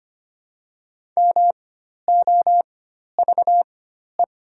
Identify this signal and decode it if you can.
Morse code, heard as a keyed tone, about 25 wpm, spaced as for 12 wpm, 700 Hz. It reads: MOVE
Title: 4x Character Spacing - Top 500 Words - 25wpm